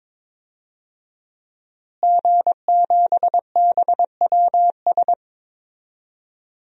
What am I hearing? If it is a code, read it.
Morse code, heard as a keyed tone, about 21 wpm, 700 Hz. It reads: G7BWS
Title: DXCC-100 Call Signs - 5 Characters - Repeated 3x - 22wpm